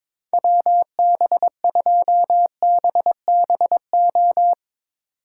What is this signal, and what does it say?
Morse code, heard as a keyed tone, about 23 wpm, 700 Hz. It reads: WB2BBO
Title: Speed Racing - USA 2x3 Call Signs 22wpm